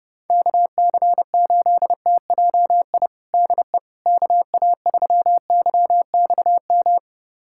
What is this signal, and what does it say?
Morse code, heard as a keyed tone, about 30 wpm, 700 Hz. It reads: KC8TJI DE KA3YXM
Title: USA Callsign DE Callsign - 6 Characters - 30wpm